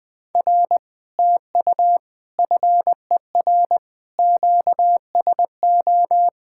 Morse code, heard as a keyed tone, about 20 wpm, 700 Hz. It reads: R TU FER QSO
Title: Speed-Racing Common QSO Phrases - 17wpm